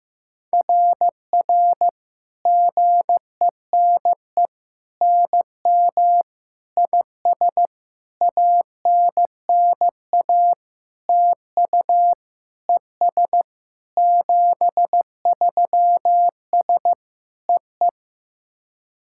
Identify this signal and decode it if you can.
Morse code, heard as a keyed tone, about 15 wpm, 700 Hz. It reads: RR GENE NM IS ANNA TU ES 73S EE
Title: SOTA QSO Practice 15wpm